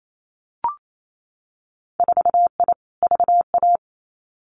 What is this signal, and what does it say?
Morse code, heard as a keyed tone, about 28 wpm, 700 Hz. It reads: 4I VA
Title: Winter Field Day Exchanges 28wpm